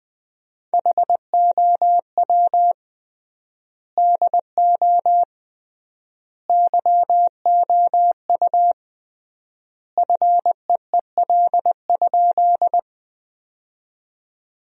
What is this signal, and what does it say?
Morse code, heard as a keyed tone, about 21 wpm, 700 Hz. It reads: HOW DO YOU FEEL?
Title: 3x Word-Spacing - Sentences from Top 400 Words - 20wpm